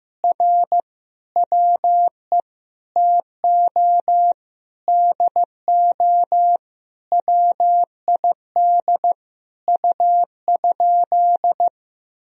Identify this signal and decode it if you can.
Morse code, heard as a keyed tone, about 15 wpm, 700 Hz. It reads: R WE TO DO WID U?
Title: Sentences from Top 100 Words and Abbreviations 15wpm